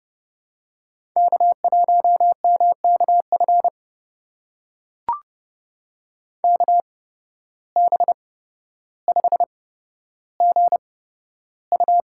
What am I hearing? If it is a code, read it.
Morse code, heard as a keyed tone, about 30 wpm, spaced as for 13 wpm, 700 Hz. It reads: K1MKF KB5GU
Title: USA Call Signs - 5 Characters - 8x Character Spacing - 30wpm